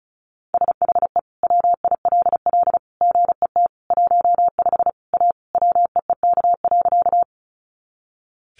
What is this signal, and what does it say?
Morse code, heard as a keyed tone, about 34 wpm, 700 Hz. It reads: SHE WILL GET 15 A WEEK.
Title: Easy Phrases 35wpm